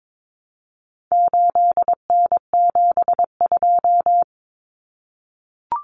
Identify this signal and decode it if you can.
Morse code, heard as a keyed tone, about 22 wpm, 700 Hz. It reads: 8N72
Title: Common Cut Numbers - 4-Digit 22wpm